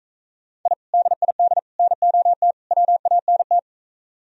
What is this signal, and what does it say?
Morse code, heard as a keyed tone, about 45 wpm, 700 Hz. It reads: I DID NOT WANT